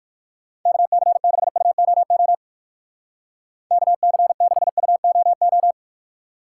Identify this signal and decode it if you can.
Morse code, heard as a keyed tone, about 57 wpm, 700 Hz. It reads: KC6UOO KC6UOO